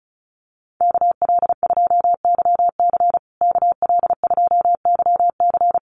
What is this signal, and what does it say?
Morse code, heard as a keyed tone, about 34 wpm, 700 Hz. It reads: KL2YC KL2YC
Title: USA Call Signs Repeated 3x 35wpm